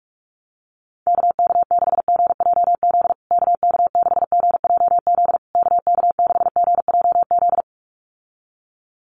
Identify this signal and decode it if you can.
Morse code, heard as a keyed tone, about 44 wpm, 690 Hz. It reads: KK6GJZ KK6GJZ KK6GJZ
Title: USA Call Signs Repeated 3x 45wpm